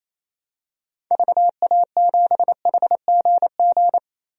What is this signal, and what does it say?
Morse code, heard as a keyed tone, about 28 wpm, 700 Hz. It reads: VA7HGG
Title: DXCC-100 Call Signs - 6 Characters - 28wpm